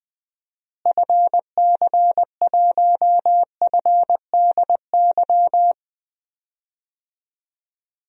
Morse code, heard as a keyed tone, about 20 wpm, 700 Hz. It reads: FC1FDY
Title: DXCC-100 Call Signs - 6 Characters - Repeated 3x - 20wpm